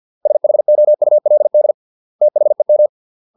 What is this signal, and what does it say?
Morse code, heard as a keyed tone, about 50 wpm, 600 Hz. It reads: SHOULD THEM